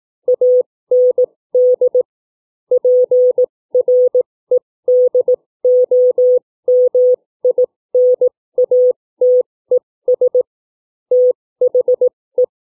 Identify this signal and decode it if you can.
Morse code, heard as a keyed tone, about 18 wpm, spaced as for 15 wpm, 500 Hz. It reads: AND PREDOMINATES THE
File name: CW / book reading 2